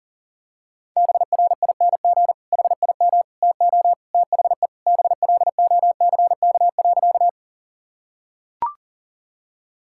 Morse code, heard as a keyed tone, about 40 wpm, 700 Hz. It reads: BRING HIM TO THE BLOCK.
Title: Sentences from Top 1600 Words 40wpm